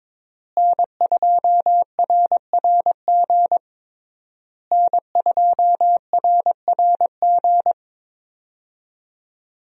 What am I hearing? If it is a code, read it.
Morse code, heard as a keyed tone, about 22 wpm, 700 Hz. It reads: N2RRG N2RRG
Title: USA Call Signs - 5 Characters - Repeated 3x - 22wpm